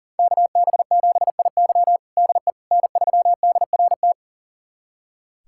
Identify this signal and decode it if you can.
Morse code, heard as a keyed tone, about 40 wpm, 700 Hz. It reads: KB7IY DE N3DRT
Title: USA Callsign DE Callsign - 5 Characters - 40wpm